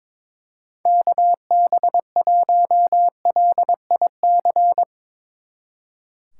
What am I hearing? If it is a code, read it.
Morse code, heard as a keyed tone, about 24 wpm, 700 Hz. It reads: KB1LIC
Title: Speed Racing - USA 2x3 Call Signs 22wpm